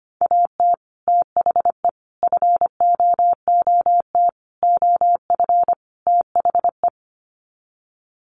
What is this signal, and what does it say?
Morse code, heard as a keyed tone, about 25 wpm, 700 Hz. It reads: AT THE FOOT OF THE
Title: Sets of 5 Words 25wpm